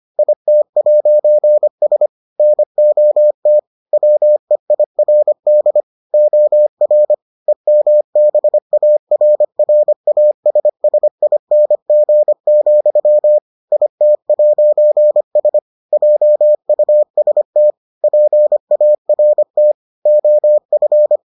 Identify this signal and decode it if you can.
Morse code, heard as a keyed tone, about 25 wpm, 600 Hz. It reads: IT'S NOT WEIRD OR EMBARRASSING, IT'S JUST PART OF